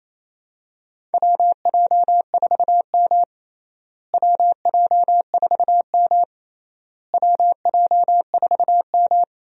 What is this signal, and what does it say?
Morse code, heard as a keyed tone, about 28 wpm, 700 Hz. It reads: WJ4M WJ4M WJ4M
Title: USA Call Signs - 4 Characters - Repeated 3x - 28wpm